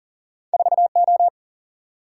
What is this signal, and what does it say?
Morse code, heard as a keyed tone, about 41 wpm, 700 Hz. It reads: 4 O
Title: Single-Letter Single-Number 40wpm